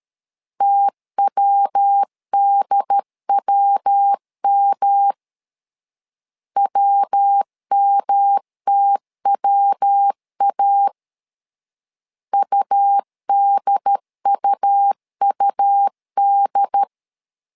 Morse code, heard as a keyed tone, about 13 wpm, 800 Hz. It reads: TWDWM WMTWA UDUUD
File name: LEZIONI / 3 MWRO / 3 MW 12wpm 1633